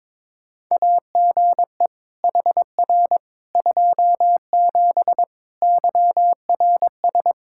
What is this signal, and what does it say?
Morse code, heard as a keyed tone, about 21 wpm, 700 Hz. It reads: AGE HR 27 YRS
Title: Practice QSO - Exchange 3 - More Info - 22wpm